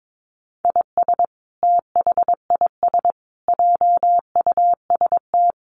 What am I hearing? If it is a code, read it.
Morse code, heard as a keyed tone, about 22 wpm, 700 Hz. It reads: IS THIS JUST
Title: Sentences from Top 1100 Words 22wpm